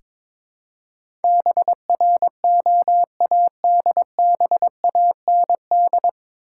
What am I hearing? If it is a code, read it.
Morse code, heard as a keyed tone, about 22 wpm, 700 Hz. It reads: BROADBAND
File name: 9 Letter Words - First 3 Letters Given - 22wpm